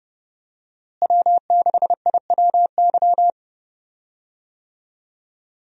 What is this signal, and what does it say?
Morse code, heard as a keyed tone, about 30 wpm, 700 Hz. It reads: W6IWY
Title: USA Call Signs - 5 Characters - Repeated 2x - 30wpm